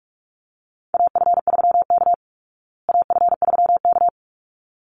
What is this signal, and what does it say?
Morse code, heard as a keyed tone, about 44 wpm, 700 Hz. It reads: AF3K AF3K
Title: Call Signs Repeated 3x - 45wpm